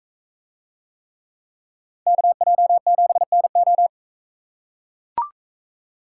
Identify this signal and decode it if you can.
Morse code, heard as a keyed tone, about 42 wpm, 700 Hz. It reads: KJ7NO